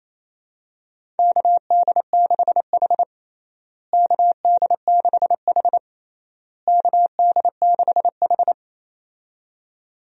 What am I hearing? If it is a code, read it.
Morse code, heard as a keyed tone, about 27 wpm, 700 Hz. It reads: KD6H KD6H KD6H
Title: USA Call Signs - 4 Characters - Repeated 3x - 28wpm